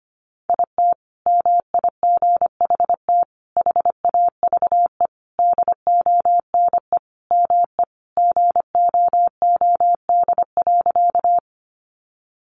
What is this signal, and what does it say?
Morse code, heard as a keyed tone, about 25 wpm, 700 Hz. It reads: IT MIGHT HAVE DONE ME GOOD.
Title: Sentences from Top 500 Words 25wpm